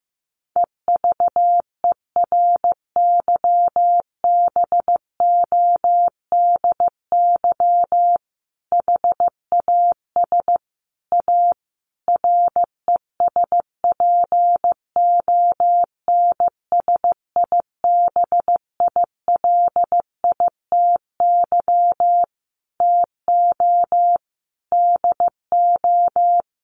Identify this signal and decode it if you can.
Morse code, heard as a keyed tone, about 15 wpm, 700 Hz. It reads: EVERYBODY HAS A RESPONSIBILITY TO DO
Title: Sentences from Top 1400 Words 15wpm